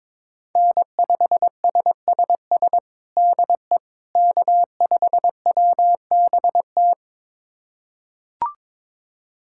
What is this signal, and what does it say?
Morse code, heard as a keyed tone, about 22 wpm, 700 Hz. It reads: N5SSS DE K5WBT
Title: USA Callsign DE Callsign - 5 Characters - 22wpm